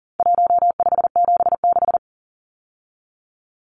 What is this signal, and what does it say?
Morse code, heard as a keyed tone, about 40 wpm, 700 Hz. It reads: 1576
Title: Numbers - 4-Digit 40wpm